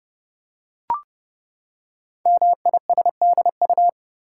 Morse code, heard as a keyed tone, about 30 wpm, 690 Hz. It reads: MISDU
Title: Code Groups - 5 Letters - 30wpm